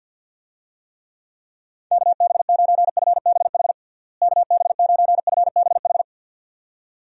Morse code, heard as a keyed tone, about 48 wpm, 700 Hz. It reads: KB9FBH KB9FBH
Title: USA Call Signs - 6 Characters - Repeated 2x - 50wpm